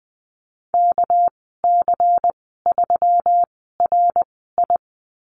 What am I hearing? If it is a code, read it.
Morse code, heard as a keyed tone, about 20 wpm, spaced as for 15 wpm, 700 Hz. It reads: KC3RI
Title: 200 Lesson 32g - Letter k - Call Signs 20wpm